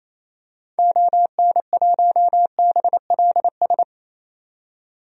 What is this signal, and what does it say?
Morse code, heard as a keyed tone, about 28 wpm, 700 Hz. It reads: ON1BLS
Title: DXCC-100 Call Signs - 6 Characters - 28wpm